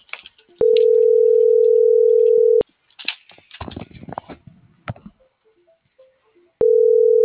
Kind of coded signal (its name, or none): telephone call progress tone